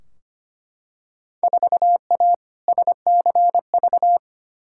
Morse code, heard as a keyed tone, about 25 wpm, 700 Hz. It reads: 4A SCV
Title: Field Day Exchanges - 25wpm